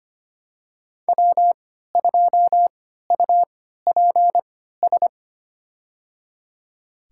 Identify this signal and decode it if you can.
Morse code, heard as a keyed tone, about 25 wpm, spaced as for 14 wpm, 700 Hz. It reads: W2UPS
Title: USA Call Signs - 5 Characters - 3x Character Spacing - 25wpm